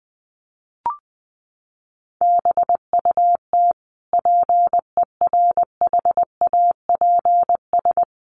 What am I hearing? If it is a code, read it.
Morse code, heard as a keyed tone, about 20 wpm, 700 Hz. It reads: BUT PERHAPS
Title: Sentences from Top 1400 Words 20wpm